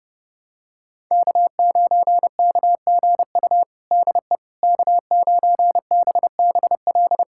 Morse code, heard as a keyed tone, about 30 wpm, 700 Hz. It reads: K9KGU DE K9BBL
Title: USA Callsign DE Callsign - 5 Characters - 30wpm